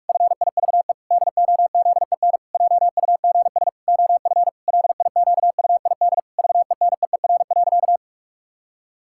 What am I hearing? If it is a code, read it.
Morse code, heard as a keyed tone, about 46 wpm, 700 Hz. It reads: FIVE DOZEN JUGS OF LIQUID VENEER.
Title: Pangram Challenge v2 - 45wpm